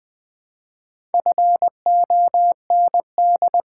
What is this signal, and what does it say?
Morse code, heard as a keyed tone, about 20 wpm, 690 Hz. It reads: FOND
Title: Build A Word - 2 Letters At A Time - 4-Letter Words - 20wpm